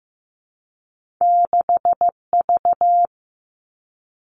Morse code, heard as a keyed tone, about 15 wpm, 700 Hz. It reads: 6V
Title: Two Challenging Letter-Number Combinations 15wpm